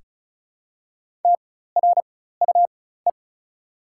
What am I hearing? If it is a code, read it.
Morse code, heard as a keyed tone, about 34 wpm, spaced as for 16 wpm, 690 Hz. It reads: TRUE
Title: 4x Character Spacing - Top 500 Words - 35wpm